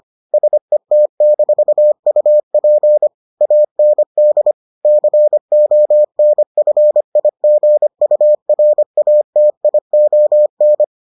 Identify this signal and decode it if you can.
Morse code, heard as a keyed tone, about 25 wpm, 600 Hz. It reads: SET-UP AND CONFIGURATION